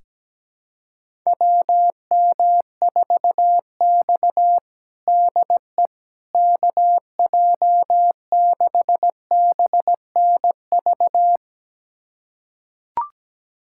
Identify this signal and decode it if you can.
Morse code, heard as a keyed tone, about 17 wpm, 700 Hz. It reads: WM4X DE KJ6BNV
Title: USA Callsign DE Callsign 17wpm